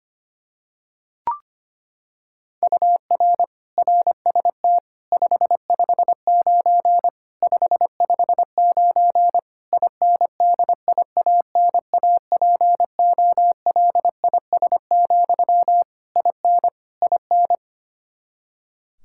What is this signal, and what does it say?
Morse code, heard as a keyed tone, about 25 wpm, 700 Hz. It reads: UR RST 559 559 INDIANAPOLIS, IN IN
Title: Practice QSO - Exchange 1 - Beginning - 25wpm